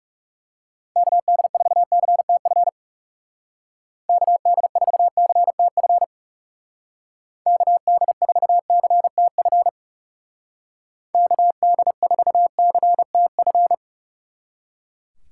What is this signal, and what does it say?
Morse code, heard as a keyed tone, about 36 wpm, 700 Hz. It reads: KD4CTF KD4CTF KD4CTF KD4CTF
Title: Speed Racing - USA 2x3 Call Signs 30wpm